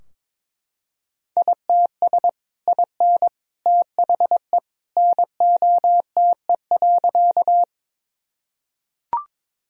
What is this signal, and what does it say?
Morse code, heard as a keyed tone, about 21 wpm, 700 Hz. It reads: ITS IN THE NOTE.
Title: Sentences from Top 1000 Words - Review Part 2 - 22wpm